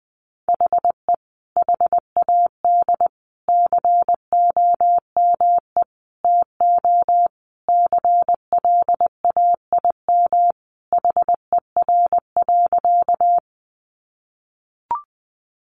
Morse code, heard as a keyed tone, about 19 wpm, 700 Hz. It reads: HE HAD COME TO CLAIM HER.
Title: Sentences from Top 1500 Words 20wpm